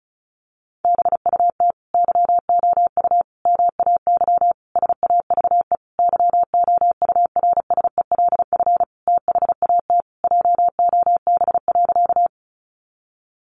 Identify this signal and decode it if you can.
Morse code, heard as a keyed tone, about 34 wpm, 700 Hz. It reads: BUT YOU MAY SAVE YOURSELF THAT JOB.
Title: Sentences from Top 1100 Words 35wpm